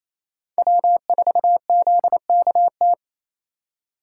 Morse code, heard as a keyed tone, about 28 wpm, 700 Hz. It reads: W4ZKT
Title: USA Call Signs - 3x Character Spacing - 28wpm